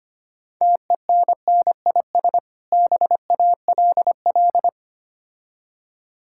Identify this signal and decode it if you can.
Morse code, heard as a keyed tone, about 25 wpm, 700 Hz. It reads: TENNIS BALL
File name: practice / 25wpm / Difficult Words with Hints - 25wpm